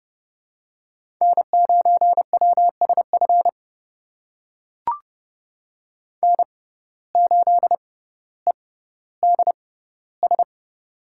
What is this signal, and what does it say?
Morse code, heard as a keyed tone, about 30 wpm, spaced as for 15 wpm, 700 Hz. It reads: N9WSF N8EDS
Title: USA Call Signs - 5 Characters - 6x Character Spacing - 30wpm